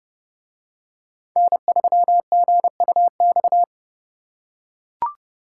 Morse code, heard as a keyed tone, about 30 wpm, 700 Hz. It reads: N3GUX